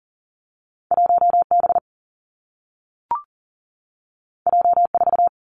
Morse code, heard as a keyed tone, about 40 wpm, 700 Hz. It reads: 1 B J 4
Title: Two Challenging Letter-Number Combinations 40wpm